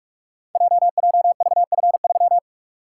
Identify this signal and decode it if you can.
Morse code, heard as a keyed tone, about 44 wpm, 690 Hz. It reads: JJVF3
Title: Code Groups - 5 Letter:Numbers - 45wpm